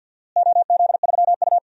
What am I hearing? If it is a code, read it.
Morse code, heard as a keyed tone, about 46 wpm, 700 Hz. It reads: OZ3U
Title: DXCC-100 Call Signs - 4 Characters - Repeated 3x - 50wpm